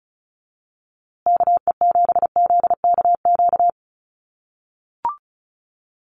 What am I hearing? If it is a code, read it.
Morse code, heard as a keyed tone, about 34 wpm, 700 Hz. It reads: KE7ZKQ
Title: USA Call Signs Repeated 2x 35wpm